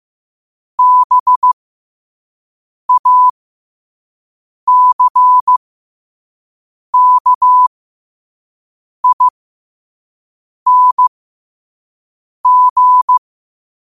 Morse code, heard as a keyed tone, about 15 wpm, spaced as for 5 wpm, 1000 Hz. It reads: BACKING
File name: qotdfiles / 5wpm